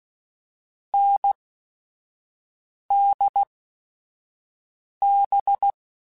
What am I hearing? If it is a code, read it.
Morse code, heard as a keyed tone, about 16 wpm, spaced as for 5 wpm, 800 Hz. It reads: NDB
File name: Short Runs / Short Runs-mp3s / NDB NDB6 @5